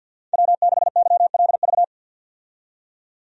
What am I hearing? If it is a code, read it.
Morse code, heard as a keyed tone, about 47 wpm, 700 Hz. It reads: W6YLV
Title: USA Call Signs - 5 Characters - 50wpm